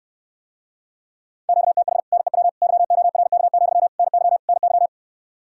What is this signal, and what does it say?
Morse code, heard as a keyed tone, about 69 wpm, 700 Hz. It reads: QTH NEW YORK, NY NY